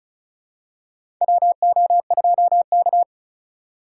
Morse code, heard as a keyed tone, about 36 wpm, 700 Hz. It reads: WO2K
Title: USA Call Signs - 4 Characters - 3x Character Spacing - 35wpm